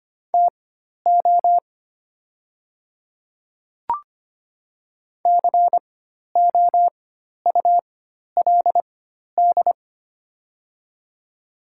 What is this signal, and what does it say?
Morse code, heard as a keyed tone, about 25 wpm, spaced as for 12 wpm, 700 Hz. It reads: TO COULD
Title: 4x Character Spacing - Top 100 Words - 25wpm